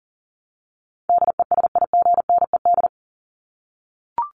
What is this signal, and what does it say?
Morse code, heard as a keyed tone, about 40 wpm, 700 Hz. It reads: DESIGNED
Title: ED Suffix - Top 100 Words with Base Words 40wpm